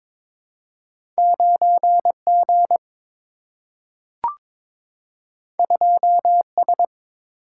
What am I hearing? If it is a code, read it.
Morse code, heard as a keyed tone, about 22 wpm, 700 Hz. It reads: 9G 2S
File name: practice / 22wpm / Two Letter-Number Combinations - 22wpm